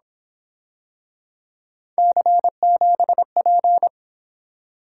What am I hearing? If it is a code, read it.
Morse code, heard as a keyed tone, about 26 wpm, 700 Hz. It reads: C7P
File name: Speed-Racing - Three Letter-Number Combinations - 22wpm